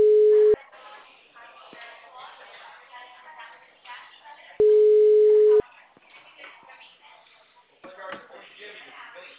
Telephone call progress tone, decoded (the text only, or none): ringback tone